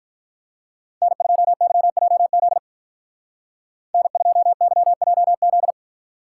Morse code, heard as a keyed tone, about 44 wpm, 700 Hz. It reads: N2YJZ N2YJZ
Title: Speed Racing - USA 1x3 Call Signs 35wpm